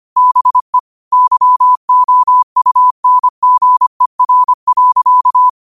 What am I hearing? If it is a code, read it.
Morse code, heard as a keyed tone, about 25 wpm, 1000 Hz. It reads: BE YOUNGER.
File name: qotdfiles / 25wpm